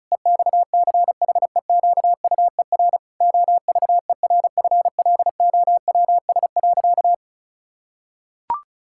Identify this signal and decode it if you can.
Morse code, heard as a keyed tone, about 36 wpm, 700 Hz. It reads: EXCHEQUER OVERFLOWS.
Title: Pangram Challenge v2 - 35wpm